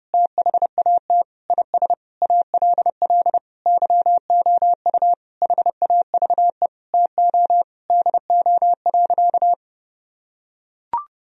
Morse code, heard as a keyed tone, about 30 wpm, 700 Hz. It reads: THAT IS ALL YOU HAVE TO DO.
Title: Sentences from Top 1000 Words - Review Part 2 - 30wpm